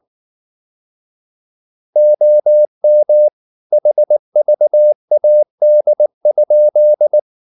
Morse code, heard as a keyed tone, about 19 wpm, 600 Hz. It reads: OM HVAD?